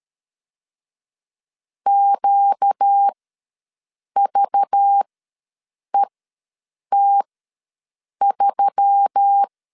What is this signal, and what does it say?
Morse code, heard as a keyed tone, about 13 wpm, spaced as for 7 wpm, 800 Hz. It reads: QVET3
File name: LEZIONI / 8 YJ45 / 8 YJ ws 12wpm 3342